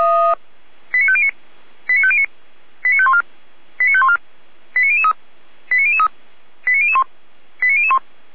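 Five-tone selective call (ZVEI), heard as quick runs of five tones, tones of about 70 ms. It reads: 89509 89509 89514 89514 89003 89003 89001 89001
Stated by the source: Five-tone ZVEI selective call received by SDR.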